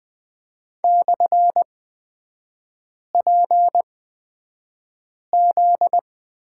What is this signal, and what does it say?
Morse code, heard as a keyed tone, about 20 wpm, spaced as for 5 wpm, 700 Hz. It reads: /PZ
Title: 243 Lesson 39e - Letter z - Three Characters 20wpm